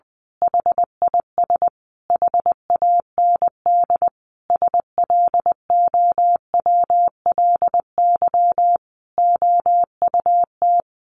Morse code, heard as a keyed tone, about 20 wpm, 700 Hz. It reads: HIS HAND SLOWLY OUT
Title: A Sound Of Thunder 20wpm